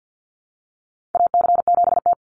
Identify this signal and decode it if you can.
Morse code, heard as a keyed tone, about 51 wpm, 700 Hz. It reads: AC7T
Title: Call Signs - 50wpm